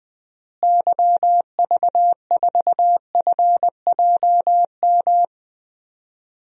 Morse code, heard as a keyed tone, about 21 wpm, 700 Hz. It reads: YV4FJM
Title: DXCC-100 Call Signs - 6 Characters - Repeated 2x - 20wpm